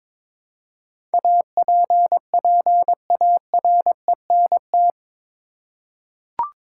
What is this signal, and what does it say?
Morse code, heard as a keyed tone, about 22 wpm, 700 Hz. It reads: APPARENT
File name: practice / 22wpm / Eight Letter Words - 22wpm